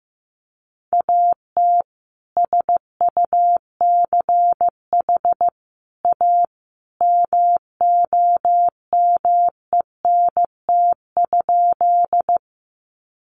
Morse code, heard as a keyed tone, about 15 wpm, 700 Hz. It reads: AT SUCH A MOMENT?
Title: Sentences from Top 1000 Words 15wpm